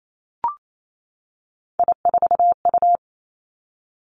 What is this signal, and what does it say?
Morse code, heard as a keyed tone, about 28 wpm, 700 Hz. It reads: I4U
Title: Three Challenging Letter-Number Combinations 28wpm